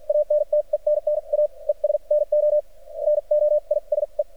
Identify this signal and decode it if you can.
Morse code, heard as a keyed tone, about 53 wpm, spaced as for 42 wpm, 600 Hz. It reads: ANTENNA ES NO NOISE
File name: archived / CRSnet